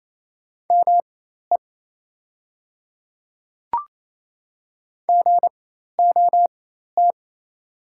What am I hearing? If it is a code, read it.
Morse code, heard as a keyed tone, about 28 wpm, spaced as for 13 wpm, 700 Hz. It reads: ME GOT
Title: 4x Character Spacing - Top 300 Words - 28wpm